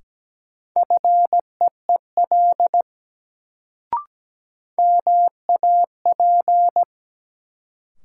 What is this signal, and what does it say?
Morse code, heard as a keyed tone, about 17 wpm, 700 Hz. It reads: FEEL MAP
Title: Sentences from Top 400 Words - New Word Drill - 17wpm